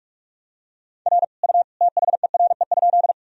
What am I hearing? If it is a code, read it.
Morse code, heard as a keyed tone, about 46 wpm, 700 Hz. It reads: R U THERE?
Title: Sentences from Top 100 Words and Abbreviations 45wpm